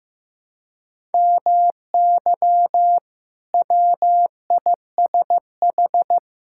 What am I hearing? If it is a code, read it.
Morse code, heard as a keyed tone, about 15 wpm, 700 Hz. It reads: MY WISH
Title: Sentences from Top 1600 Words 15wpm